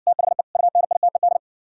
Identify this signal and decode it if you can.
Morse code, heard as a keyed tone, about 58 wpm, 700 Hz. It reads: THE UNITED